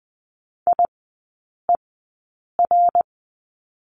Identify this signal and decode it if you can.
Morse code, heard as a keyed tone, about 20 wpm, spaced as for 8 wpm, 700 Hz. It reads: IER
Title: 053 Lesson 10e - Letter r - Three Characters 20wpm